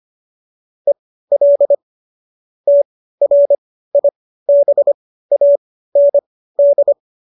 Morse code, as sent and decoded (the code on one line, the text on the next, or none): . .-.. / - .-. .. -... .- -. -..
EL TRIBAND